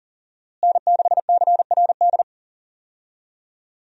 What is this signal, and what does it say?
Morse code, heard as a keyed tone, about 37 wpm, 700 Hz. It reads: N6CRD
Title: USA Call Signs - 5 Characters - Repeated 3x - 40wpm